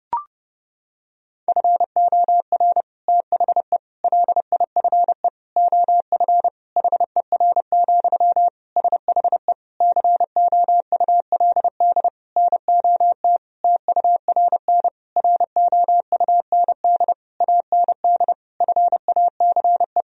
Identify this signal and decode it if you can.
Morse code, heard as a keyed tone, about 32 wpm, 700 Hz. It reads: FOR THE LIFE OF HER, SHE COULD NOT TURN ROUND AND FACE
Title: Sentences from Top 1000 Words - Review Part 2 - 30wpm